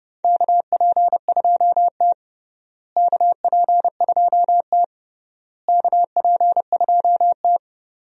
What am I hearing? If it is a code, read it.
Morse code, heard as a keyed tone, about 29 wpm, 700 Hz. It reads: KP2T KP2T KP2T
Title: DXCC-100 Call Signs - 4 Characters - Repeated 3x - 30wpm